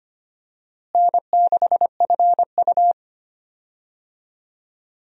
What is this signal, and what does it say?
Morse code, heard as a keyed tone, about 25 wpm, 700 Hz. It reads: N6FU
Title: USA Call Signs - 4 Characters - Repeated 3x - 25wpm